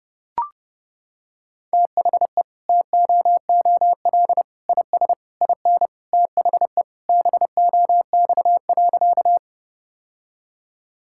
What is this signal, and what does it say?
Morse code, heard as a keyed tone, about 30 wpm, 700 Hz. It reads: THE TOOL IS IN THE BOX.
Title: Sentences from Top 1000 Words - Review Part 6 - 30wpm